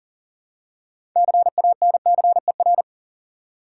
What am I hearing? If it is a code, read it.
Morse code, heard as a keyed tone, about 40 wpm, 700 Hz. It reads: CANCER
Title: Six Letter Words - 40wpm